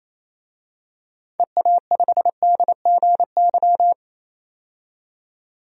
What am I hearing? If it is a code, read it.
Morse code, heard as a keyed tone, about 28 wpm, 700 Hz. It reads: EA5DGY
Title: DXCC-100 Call Signs - 6 Characters - 28wpm